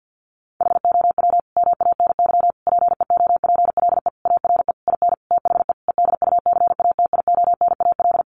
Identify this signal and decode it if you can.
Morse code, heard as a keyed tone, about 50 wpm, 700 Hz. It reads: HOW MANY PEOPLE ARE IN THE EDUCATIONAL
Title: Sentences from Top 1200 Words 50wpm